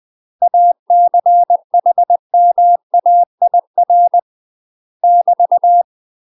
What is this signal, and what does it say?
Morse code, heard as a keyed tone, about 20 wpm, 700 Hz. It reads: ACHMAIR <BT>
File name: UK MASTER QSOs TRAINING / 20WPM / UK MASTER QSOs 0010